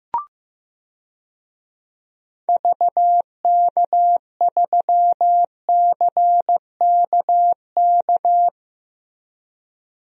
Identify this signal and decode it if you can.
Morse code, heard as a keyed tone, about 15 wpm, 700 Hz. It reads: VK3CKK
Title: DXCC-100 Call Signs - 6 Characters - Repeated 3x - 15wpm